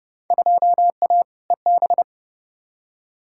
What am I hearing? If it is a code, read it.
Morse code, heard as a keyed tone, about 32 wpm, 700 Hz. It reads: 2A EB
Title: Field Day Exchanges - 30wpm